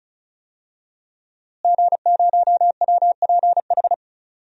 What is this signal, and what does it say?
Morse code, heard as a keyed tone, about 36 wpm, 700 Hz. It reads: G0WPH
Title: DXCC-100 Call Signs - 5 Characters - Repeated 3x - 35wpm